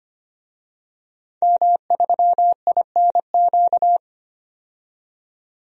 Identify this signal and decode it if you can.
Morse code, heard as a keyed tone, about 25 wpm, 700 Hz. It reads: M3INQ
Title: DXCC-100 Call Signs - 5 Characters - 25wpm